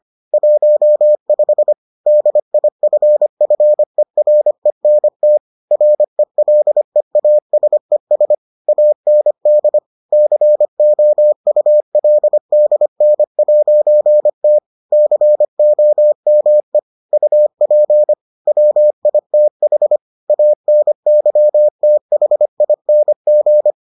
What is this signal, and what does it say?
Morse code, heard as a keyed tone, about 25 wpm, 600 Hz. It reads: 15 DIFFERENT RELEASES AND COULDN'T COME UP WITH ANYTHING